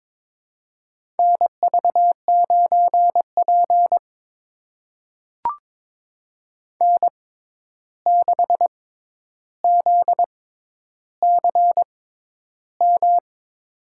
Morse code, heard as a keyed tone, about 22 wpm, spaced as for 11 wpm, 700 Hz. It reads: NV9P N6ZCM
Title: USA Call Signs - 6x Character Spacing - 22wpm